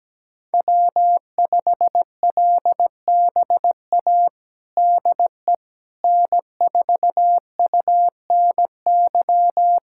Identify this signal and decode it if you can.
Morse code, heard as a keyed tone, about 17 wpm, 700 Hz. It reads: W5LBA DE N4UNY
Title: USA Callsign DE Callsign - 5 Characters - 17wpm